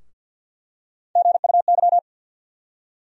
Morse code, heard as a keyed tone, about 47 wpm, 700 Hz. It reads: GUY
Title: USA Call Sign Postfixes - 3 Letters 50wpm